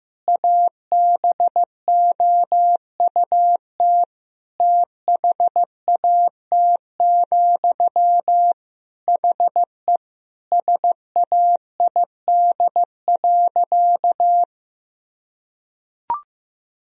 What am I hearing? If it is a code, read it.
Morse code, heard as a keyed tone, about 15 wpm, 700 Hz. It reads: ABOUT THAT, HE SAID.
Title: Sentences from Top 1000 Words - Review Part 3 - 15wpm